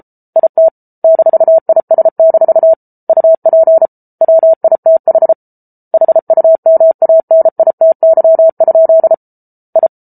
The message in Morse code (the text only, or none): IT -IS- UP WITH HUMANITY? I